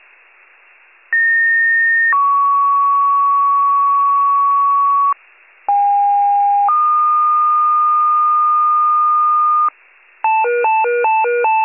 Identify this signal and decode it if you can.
Two-tone page, heard as a one-second tone, then a three-second tone, about 1850 then 1120 Hz, about 810 then 1220 Hz. It is heard from 1.1 to 9.7 s.